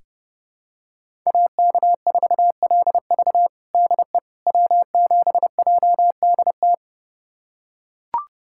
Morse code, heard as a keyed tone, about 30 wpm, 700 Hz. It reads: AK4LV DE W7JDT
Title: USA Callsign DE Callsign - 5 Characters - 30wpm